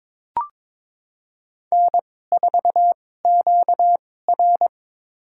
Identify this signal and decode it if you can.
Morse code, heard as a keyed tone, about 22 wpm, spaced as for 16 wpm, 700 Hz. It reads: N4QR
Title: USA Call Signs - 2x Character Spacing - 22wpm